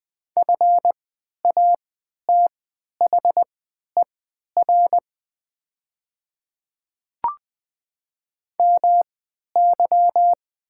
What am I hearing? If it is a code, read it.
Morse code, heard as a keyed tone, about 20 wpm, spaced as for 11 wpm, 700 Hz. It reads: FATHER MY
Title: 3x Character Spacing - Top 200 Words - 20wpm